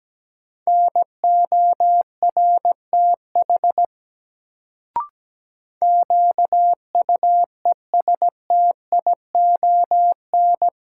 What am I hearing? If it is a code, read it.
Morse code, heard as a keyed tone, about 17 wpm, 700 Hz. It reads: NORTH QUESTION
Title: Sentences from Top 400 Words - New Word Drill - 17wpm